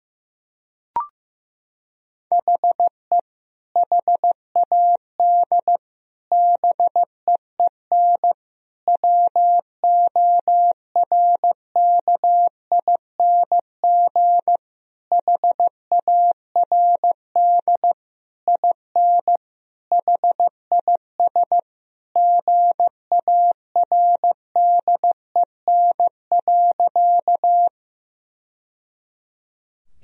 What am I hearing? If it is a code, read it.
Morse code, heard as a keyed tone, about 15 wpm, 700 Hz. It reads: HE HAD BEEN WORKING HARD IN HIS GARDEN.
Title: Sentences from Top 1000 Words - Review Part 4 - 15wpm